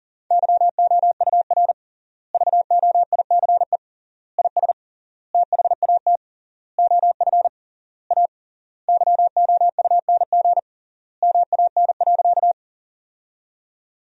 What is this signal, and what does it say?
Morse code, heard as a keyed tone, about 40 wpm, 700 Hz. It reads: YOUR VOICE IS THAT OF A YOUNG MAN.
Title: 3x Word-Spacing - Sentences from Top 500 Words - 40wpm